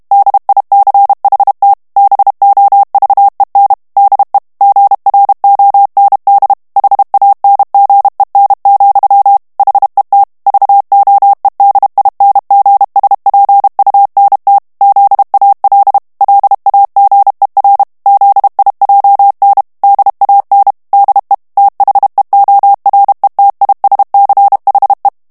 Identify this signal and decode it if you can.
Morse code, heard as a keyed tone, about 32 wpm, 800 Hz. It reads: DICHT BOVEN DE GROND HANGEN, HET VOEDINGSPUNT ZAL LAGER ZIJN DAN DE THEORETISCHE
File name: HOLLANDAIS texte clair 1 vitesse 30 mots minute